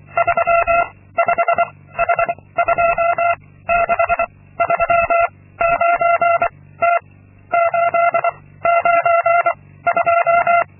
Morse code, heard as a keyed tone, about 24 wpm, 680 Hz. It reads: 3 5 H 2 6 3 9 T 8 9 2
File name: Linksmakalnis-2